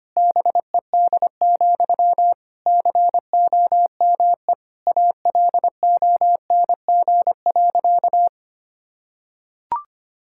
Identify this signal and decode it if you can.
Morse code, heard as a keyed tone, about 25 wpm, 700 Hz. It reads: BED, COME ALONG.
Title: Sentences from Top 1600 Words 25wpm